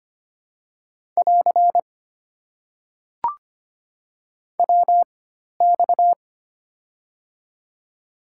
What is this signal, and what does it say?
Morse code, heard as a keyed tone, about 25 wpm, spaced as for 12 wpm, 690 Hz. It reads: <AR> WX
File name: Top 90 QSO Elements - 4x Character Spacing - 25wpm